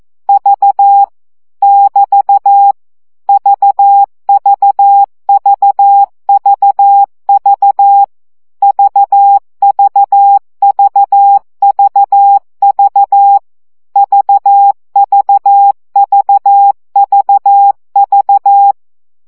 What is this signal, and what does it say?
Morse code, heard as a keyed tone, about 14 wpm, 800 Hz. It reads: V <BT> VVVVV VVVVV VVVVV
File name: CW gyakorlas v 12 baud